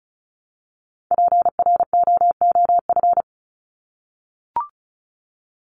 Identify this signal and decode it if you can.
Morse code, heard as a keyed tone, about 34 wpm, 700 Hz. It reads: PROOF
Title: Top 1501-1750 Words 35wpm